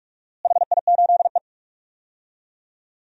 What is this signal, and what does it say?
Morse code, heard as a keyed tone, about 41 wpm, 700 Hz. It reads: HI8E
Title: DXCC-100 Call Signs - 4 Characters - Repeated 2x - 45wpm